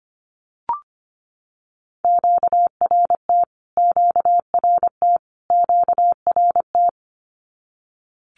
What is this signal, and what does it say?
Morse code, heard as a keyed tone, about 25 wpm, 700 Hz. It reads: QRT QRT QRT
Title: Q-Codes Repeated 3x 25wpm